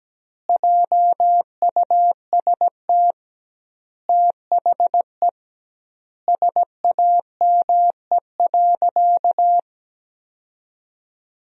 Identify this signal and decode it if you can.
Morse code, heard as a keyed tone, about 17 wpm, 700 Hz. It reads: JUST THE SAME.
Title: 2x Word-Spacing - Sentences from Top 300 Words - 17wpm